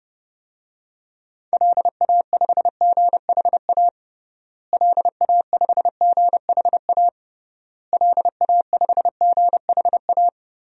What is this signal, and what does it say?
Morse code, heard as a keyed tone, about 32 wpm, 700 Hz. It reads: LA5GHA LA5GHA LA5GHA
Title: DXCC-100 Call Signs - 6 Characters - Repeated 3x - 30wpm